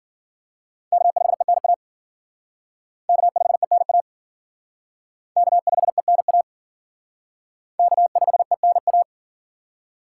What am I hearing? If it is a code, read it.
Morse code, heard as a keyed tone, about 50 wpm, 700 Hz. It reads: K5ENA K5ENA K5ENA K5ENA
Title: Speed Racing - USA 1x3 Call Signs 40wpm